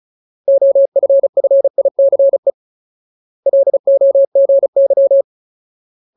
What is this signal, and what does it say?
Morse code, heard as a keyed tone, about 34 wpm, 550 Hz. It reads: OFFICE LOGY
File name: Ultimate Word List / 35WPM Ultimate